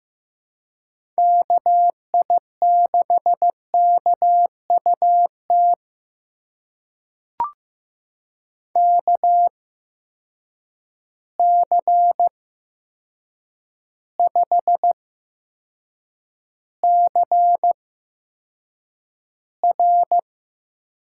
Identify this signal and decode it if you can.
Morse code, heard as a keyed tone, about 15 wpm, spaced as for 7 wpm, 700 Hz. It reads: KI6KUT KC5CR
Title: USA Call Signs - 6 Characters - 8x Character Spacing - 15wpm